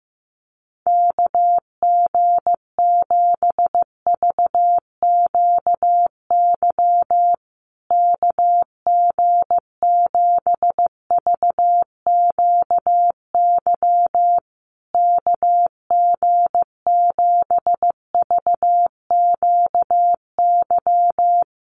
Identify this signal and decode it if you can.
Morse code, heard as a keyed tone, about 15 wpm, 700 Hz. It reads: KG7VQY KG7VQY KG7VQY
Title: USA Call Signs Repeated 3x 15wpm